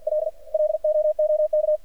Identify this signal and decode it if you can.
Morse code, heard as a keyed tone, about 51 wpm, 610 Hz, with radio noise. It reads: L LOOK
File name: archived / CRSnet